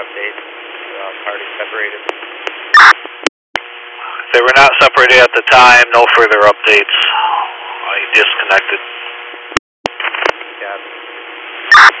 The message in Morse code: none

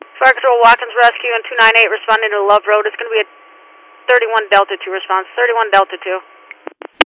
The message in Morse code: none

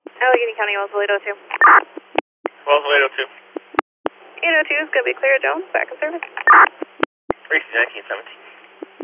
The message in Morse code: none

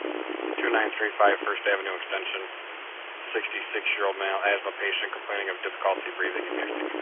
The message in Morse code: none

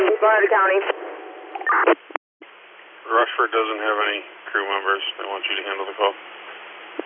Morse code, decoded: none

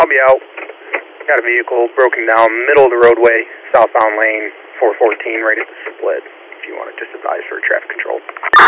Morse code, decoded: none

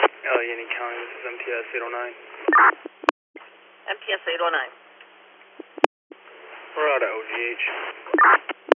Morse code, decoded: none

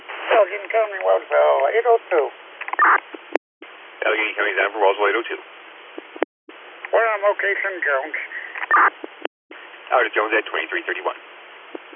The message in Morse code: none

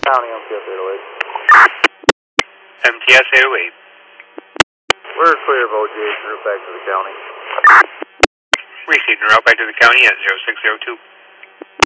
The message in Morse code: none

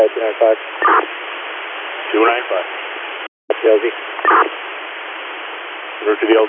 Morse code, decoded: none

